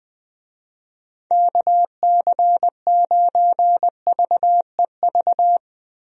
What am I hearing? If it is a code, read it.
Morse code, heard as a keyed tone, about 21 wpm, 700 Hz. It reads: KC9VEV
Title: USA Call Signs - 6 Characters - 20wpm